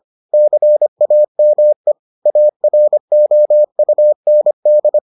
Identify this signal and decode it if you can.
Morse code, heard as a keyed tone, about 25 wpm, 600 Hz. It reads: CAME AROUND